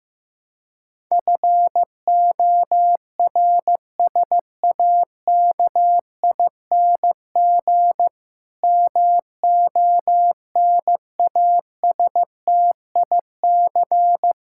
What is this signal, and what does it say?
Morse code, heard as a keyed tone, about 15 wpm, 700 Hz. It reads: FORSAKING MONASTIC
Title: Pangram Challenge v2 - 15wpm